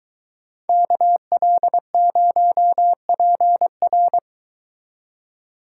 Morse code, heard as a keyed tone, about 23 wpm, 700 Hz. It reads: KL0PR